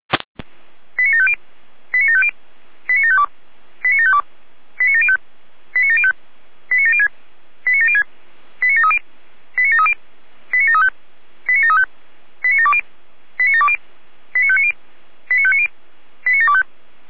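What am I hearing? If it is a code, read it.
Five-tone selective call (ZVEI), heard as quick runs of five tones, tones of about 70 ms. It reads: 89750 89750 89752 89752 89705 89705 89706 89706 89730 89730 89735 89735 89720 89720 89590 89590 89725
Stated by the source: ZVEI five-tone selective call 89531 received by SDR.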